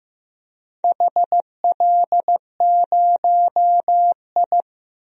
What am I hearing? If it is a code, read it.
Morse code, heard as a keyed tone, about 15 wpm, 700 Hz. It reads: HL0I
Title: DXCC-100 Call Signs - 4 Characters - 15wpm